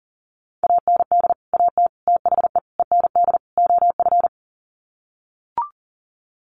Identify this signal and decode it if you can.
Morse code, heard as a keyed tone, about 40 wpm, 700 Hz. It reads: AND AT THE END OF
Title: Sets of 5 Words 40wpm